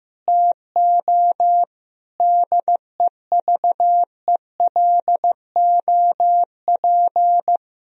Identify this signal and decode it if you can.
Morse code, heard as a keyed tone, about 16 wpm, 700 Hz. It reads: TO DEVELOP
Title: Sentences from Top 1000 Words - Review Part 3 - 15wpm